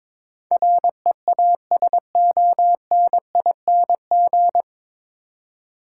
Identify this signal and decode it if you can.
Morse code, heard as a keyed tone, about 22 wpm, 700 Hz. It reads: REASONING
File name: 9 Letter Words - First 3 Letters Given - 22wpm